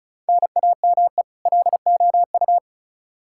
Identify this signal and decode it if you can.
Morse code, heard as a keyed tone, about 34 wpm, 700 Hz. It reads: NAME LOU
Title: Practice QSO - Exchange 1 - Beginning - 35wpm